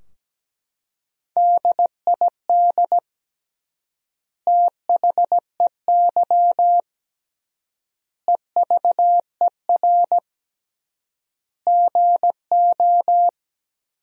Morse code, heard as a keyed tone, about 17 wpm, 700 Hz. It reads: DID THEY EVER GO
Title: 3x Word-Spacing - Sentences from Top 400 Words - 17wpm